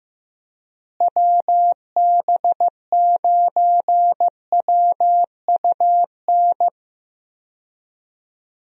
Morse code, heard as a keyed tone, about 15 wpm, 700 Hz. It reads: WB9WUN
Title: USA Call Signs - 6 Characters - Repeated 2x - 15wpm